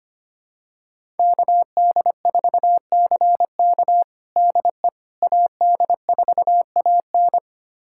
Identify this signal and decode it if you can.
Morse code, heard as a keyed tone, about 25 wpm, 700 Hz. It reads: KD4CK DE AD4AN
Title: USA Callsign DE Callsign - 5 Characters - 25wpm